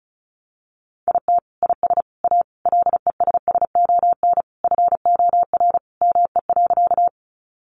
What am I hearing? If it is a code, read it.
Morse code, heard as a keyed tone, about 34 wpm, 700 Hz. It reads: IT IS A LESSON FOR ME.
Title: Sentences from Top 1200 Words 35wpm